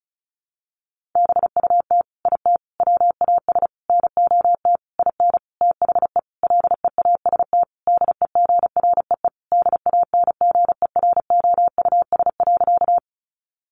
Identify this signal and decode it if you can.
Morse code, heard as a keyed tone, about 34 wpm, 700 Hz. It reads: BUT IT WAS NOT IN THE LEAST DEGREE DANGEROUS.
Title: Sentences from Top 1500 Words 35wpm